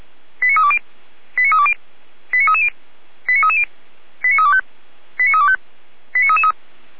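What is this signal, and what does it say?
Five-tone selective call (ZVEI), heard as quick runs of five tones, tones of about 70 ms. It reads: 89320 89320 89330 89330 89325 89325 89333
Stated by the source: ZVEI five-tone call 89315 received on an SDR.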